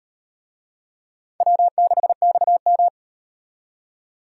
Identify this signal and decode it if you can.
Morse code, heard as a keyed tone, about 38 wpm, 700 Hz. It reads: W6XM